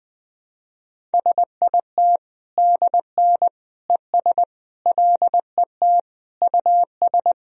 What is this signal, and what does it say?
Morse code, heard as a keyed tone, about 21 wpm, 700 Hz. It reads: SIT DN ES LET US
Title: Sentences from Top 500 Words and Abbreviations 20wpm